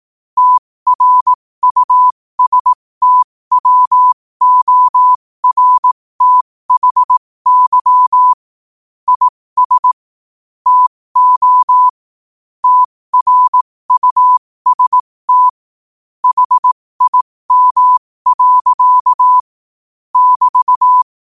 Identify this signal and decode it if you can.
Morse code, heard as a keyed tone, about 18 wpm, spaced as for 15 wpm, 1000 Hz. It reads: TRUSTWORTHY IS TO TRUST HIM. <BT>
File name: qotdfiles / 15wpm